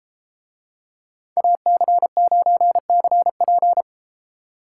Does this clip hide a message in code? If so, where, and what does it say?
Yes: Morse code from 1.4 to 3.8 s, heard as a keyed tone, about 32 wpm, 700 Hz. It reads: AC9CP